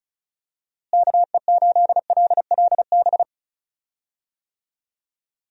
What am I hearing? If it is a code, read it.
Morse code, heard as a keyed tone, about 36 wpm, 700 Hz. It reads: KE8LLB
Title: USA Call Signs - 6 Characters - Repeated 2x - 35wpm